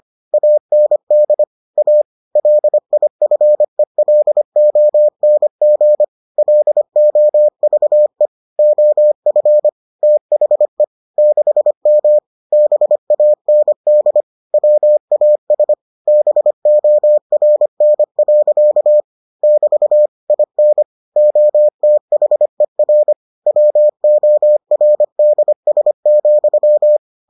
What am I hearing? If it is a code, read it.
Morse code, heard as a keyed tone, about 25 wpm, 600 Hz. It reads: AND A LIFELONG LOVE OF THE 6M BAND WAS BORN. <BT> IN OTHER WORDS,